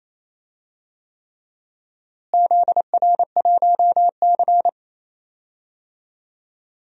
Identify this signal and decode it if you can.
Morse code, heard as a keyed tone, about 28 wpm, 700 Hz. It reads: ZR1C
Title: DXCC-100 Call Signs - 4 Characters - Repeated 3x - 28wpm